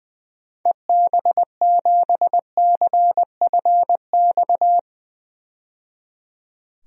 Morse code, heard as a keyed tone, about 21 wpm, 700 Hz. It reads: EB7CFX
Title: DXCC-100 Call Signs - 6 Characters - 20wpm